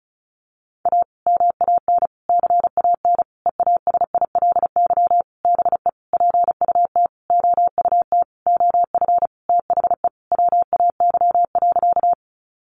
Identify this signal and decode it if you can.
Morse code, heard as a keyed tone, about 34 wpm, 700 Hz. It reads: A MAN CAN EASILY BE PUT OUT OF THE WAY.
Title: Sentences from Top 900 Words 35wpm